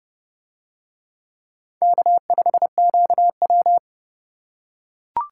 Morse code, heard as a keyed tone, about 27 wpm, 700 Hz. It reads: K5QW